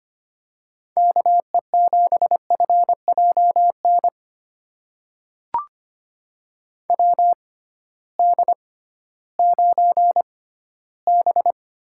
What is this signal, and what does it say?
Morse code, heard as a keyed tone, about 25 wpm, spaced as for 15 wpm, 700 Hz. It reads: KE7FJN WD9B
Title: USA Call Signs - 6 Characters - 6x Character Spacing - 25wpm